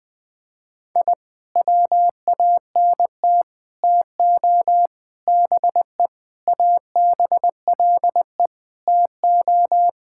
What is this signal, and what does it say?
Morse code, heard as a keyed tone, about 20 wpm, 700 Hz. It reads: I WANT TO BE ABLE TO
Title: Sentences with 3-Word Phrases - Spoken Beginning - 20wpm